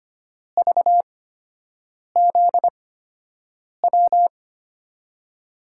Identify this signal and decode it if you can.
Morse code, heard as a keyed tone, about 25 wpm, spaced as for 7 wpm, 700 Hz. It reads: VZW
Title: USA Call Signs - 5 Characters - 8x Character Spacing - 25wpm